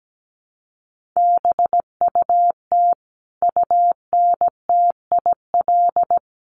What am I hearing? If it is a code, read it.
Morse code, heard as a keyed tone, about 17 wpm, 700 Hz. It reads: BUT UNTIL
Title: Sentences from Top 1200 Words 17wpm